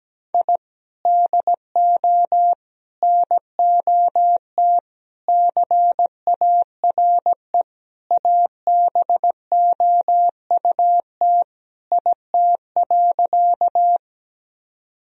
Morse code, heard as a keyed tone, about 17 wpm, 700 Hz. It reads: I DO NOT CARE ABOUT IT.
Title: Sentences from Top 1000 Words - Review Part 1 - 17wpm